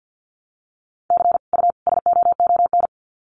Morse code, heard as a keyed tone, about 50 wpm, 700 Hz. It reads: C U SOON
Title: Common QSO Phrases 50wpm